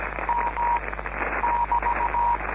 Morse code, heard as a keyed tone, about 17 wpm, 950 Hz, with noise. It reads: M <BT>